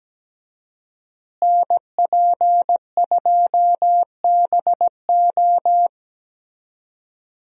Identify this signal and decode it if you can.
Morse code, heard as a keyed tone, about 17 wpm, 700 Hz. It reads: NP2BO
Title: USA Call Signs - 5 Characters - Repeated 2x - 17wpm